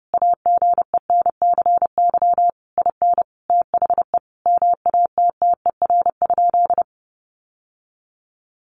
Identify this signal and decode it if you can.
Morse code, heard as a keyed tone, about 30 wpm, 700 Hz. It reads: AGENCY IN THE MATTER?
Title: Sentences from Top 1300 Words 30wpm